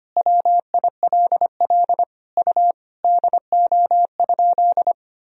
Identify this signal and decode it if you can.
Morse code, heard as a keyed tone, about 25 wpm, 700 Hz. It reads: WILL U DO?
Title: Sentences from Top 100 Words and Abbreviations 25wpm